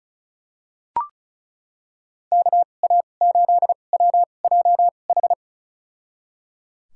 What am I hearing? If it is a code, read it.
Morse code, heard as a keyed tone, about 36 wpm, spaced as for 25 wpm, 700 Hz. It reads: KA8WJH
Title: USA Call Signs - 6 Characters - 2x Character Spacing - 35wpm